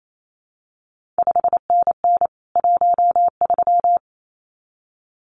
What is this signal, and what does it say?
Morse code, heard as a keyed tone, about 28 wpm, 700 Hz. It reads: 5NN 13
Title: CQ WW DX Contest Exchange Encore - 28wpm